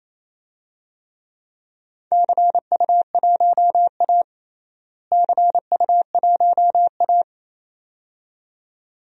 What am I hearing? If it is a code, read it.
Morse code, heard as a keyed tone, about 27 wpm, 700 Hz. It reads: CU1A CU1A
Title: DXCC-100 Call Signs - 4 Characters - Repeated 2x - 28wpm